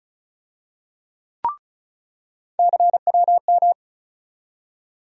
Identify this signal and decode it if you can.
Morse code, heard as a keyed tone, about 36 wpm, 700 Hz. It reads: CWM
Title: USA Call Sign Postfixes - 3 Letters 35wpm